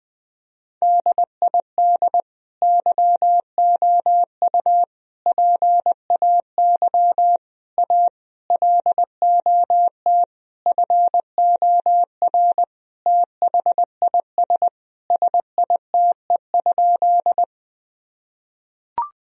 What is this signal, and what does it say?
Morse code, heard as a keyed tone, about 20 wpm, 700 Hz. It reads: DID YOU PAY A LOT FOR THIS SITE?
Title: Sentences from Top 1000 Words - Review Part 4 - 20wpm